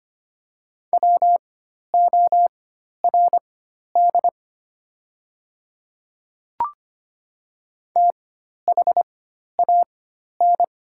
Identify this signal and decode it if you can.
Morse code, heard as a keyed tone, about 25 wpm, spaced as for 12 wpm, 690 Hz. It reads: WORD THAN
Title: 4x Character Spacing - Top 100 Words - 25wpm